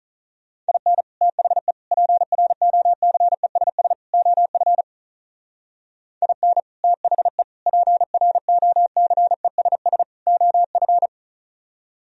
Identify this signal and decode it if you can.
Morse code, heard as a keyed tone, about 38 wpm, 700 Hz. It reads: IN THE PROCESS OF IN THE PROCESS OF
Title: Speed-Racing - Sets of 4 Words 35wpm